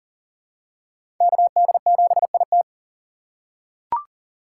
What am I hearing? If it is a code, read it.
Morse code, heard as a keyed tone, about 40 wpm, 700 Hz. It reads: KD7IT
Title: USA Call Signs - 3x Character Spacing - 40wpm